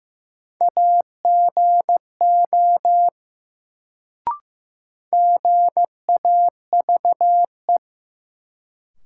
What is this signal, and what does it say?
Morse code, heard as a keyed tone, about 16 wpm, 700 Hz. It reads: AGO GAVE
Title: Sentences from Top 500 Words - New Word Drill - 15wpm